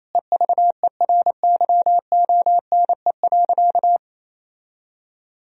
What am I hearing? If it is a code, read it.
Morse code, heard as a keyed tone, about 28 wpm, 700 Hz. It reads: EVERYONE.
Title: Sentences from Top 1000 Words - Review Part 3 - 28wpm